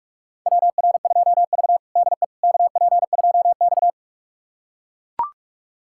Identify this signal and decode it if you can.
Morse code, heard as a keyed tone, about 44 wpm, 700 Hz. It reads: WR2V DE KP2X
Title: USA Callsign DE Callsign - 4 Characters - 45wpm